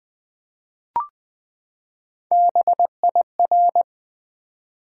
Morse code, heard as a keyed tone, about 20 wpm, 700 Hz. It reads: BIR
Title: Build A Word - 3 Letters At A Time - 8-Letter Words - 20wpm